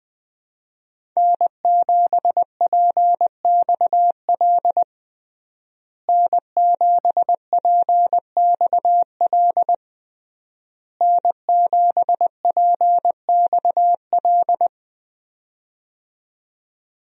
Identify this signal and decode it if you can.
Morse code, heard as a keyed tone, about 20 wpm, 700 Hz. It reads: N7PXL N7PXL N7PXL
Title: USA Call Signs - 5 Characters - Repeated 3x - 20wpm